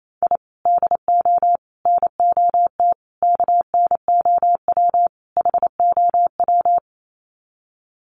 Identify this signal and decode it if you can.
Morse code, heard as a keyed tone, about 28 wpm, 700 Hz. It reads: I DO NOT KNOW HOW
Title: Sets of 5 Words 28wpm